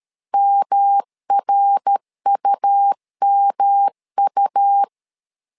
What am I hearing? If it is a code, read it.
Morse code, heard as a keyed tone, about 13 wpm, 800 Hz. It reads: MRUMU
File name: LEZIONI / 3 MWRO / 3 MWRO 12wpm 1730